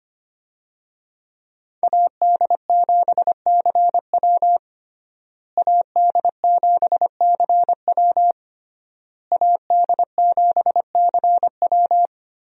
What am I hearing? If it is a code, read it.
Morse code, heard as a keyed tone, about 25 wpm, 700 Hz. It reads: AD7CW AD7CW AD7CW
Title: USA Call Signs - 5 Characters - Repeated 3x - 25wpm